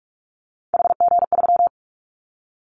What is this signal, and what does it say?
Morse code, heard as a keyed tone, about 46 wpm, 690 Hz. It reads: HG3
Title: Callsign Prefixes for Top 50 DXCC - 45wpm